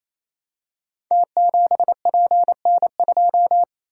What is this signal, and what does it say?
Morse code, heard as a keyed tone, about 27 wpm, 700 Hz. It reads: T7PN2
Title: Code Groups - 5 Letter:Numbers - 28wpm